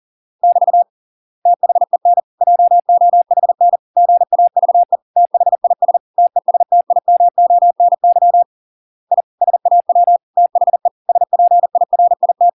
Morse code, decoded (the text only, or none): <BT> THEN JOHN GAVE THIS TESTIMONY I SAW THE SPIRIT